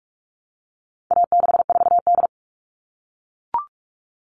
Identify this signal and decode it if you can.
Morse code, heard as a keyed tone, about 44 wpm, 700 Hz. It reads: A64D
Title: All Cut Numbers - 4-Digit 45wpm